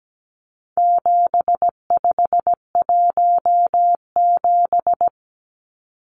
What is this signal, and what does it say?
Morse code, heard as a keyed tone, about 17 wpm, 700 Hz. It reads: 7517
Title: Numbers - 4-Digit 17wpm